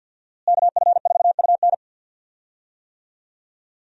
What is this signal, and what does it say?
Morse code, heard as a keyed tone, about 52 wpm, 700 Hz. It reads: KF4UN
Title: USA Call Signs - 5 Characters - Repeated 3x - 50wpm